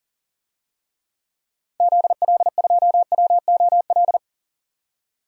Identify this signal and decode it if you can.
Morse code, heard as a keyed tone, about 40 wpm, 700 Hz. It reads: ZL2WOL
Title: DXCC-100 Call Signs - 6 Characters - 40wpm